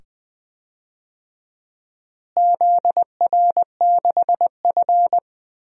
Morse code, heard as a keyed tone, about 20 wpm, 700 Hz. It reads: ZR6F
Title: DXCC-100 Call Signs - 4 Characters - Repeated 2x - 20wpm